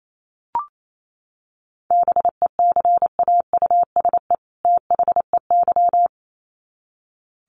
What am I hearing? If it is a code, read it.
Morse code, heard as a keyed tone, about 28 wpm, 700 Hz. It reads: BECAUSE THEY
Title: Sets of 2 Words 28wpm